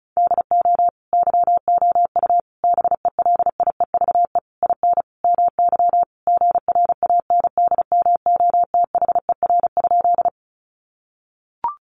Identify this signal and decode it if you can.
Morse code, heard as a keyed tone, about 34 wpm, 700 Hz. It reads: DO YOU BELIEVE IN MY GRANDMOTHER?
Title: Sentences from Top 1200 Words 35wpm